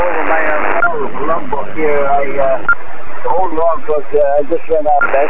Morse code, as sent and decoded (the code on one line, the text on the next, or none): none